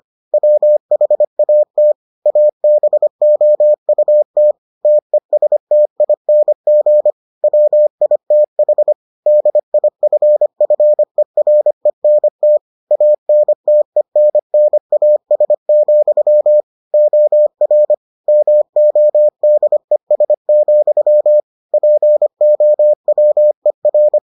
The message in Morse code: WHAT ABOUT TESTING WITH DIFFERENT ANTENNAS, OR MODES, POWER

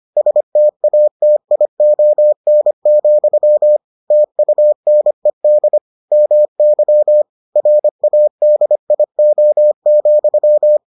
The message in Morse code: STATION, TUNED MY RADIO,

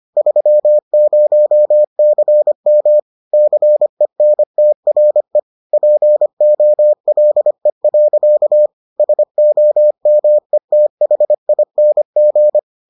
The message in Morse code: 30CM CENTRE POLE. SOMETHING

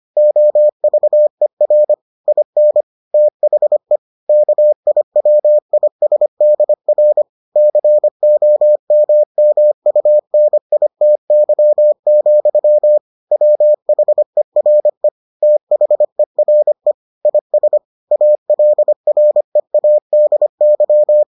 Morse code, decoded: OVER IN THE KIWISDR COMMUNITY, WHERE THERE IS ALREADY

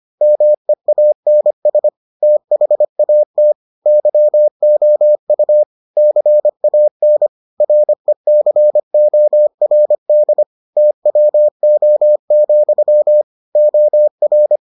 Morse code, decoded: MEANS THAT YOU CAN RECORD TWO, OR

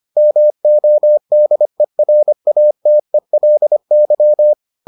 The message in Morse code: MODERATELY

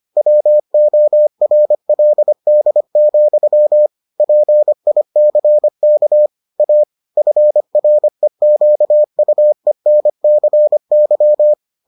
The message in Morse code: WORLD, PICK A FREQUENCY

